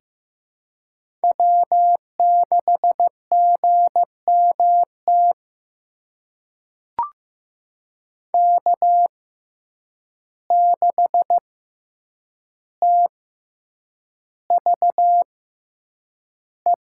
Morse code, heard as a keyed tone, about 15 wpm, spaced as for 8 wpm, 700 Hz. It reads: W6GMT K6TVE
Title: USA Call Signs - 5 Characters - 6x Character Spacing - 15wpm